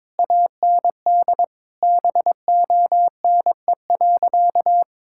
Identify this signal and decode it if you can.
Morse code, heard as a keyed tone, about 22 wpm, 700 Hz. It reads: AND BONE.
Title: Sentences from Top 1600 Words 22wpm